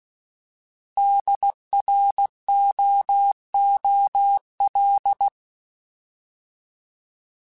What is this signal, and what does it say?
Morse code, heard as a keyed tone, about 16 wpm, 800 Hz. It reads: DROOL